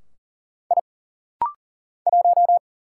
Morse code, heard as a keyed tone, about 40 wpm, 700 Hz. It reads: I 1
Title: Single Letter-Number - Rapid-Fire 40wpm